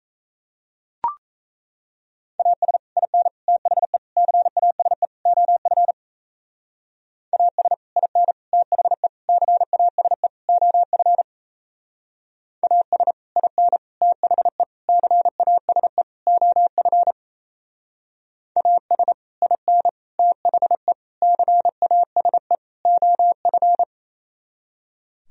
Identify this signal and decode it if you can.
Morse code, heard as a keyed tone, about 35 wpm, 700 Hz. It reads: AS IN THE CASE OF AS IN THE CASE OF AS IN THE CASE OF AS IN THE CASE OF
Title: Speed-Racing - Sets of 5 Words 28wpm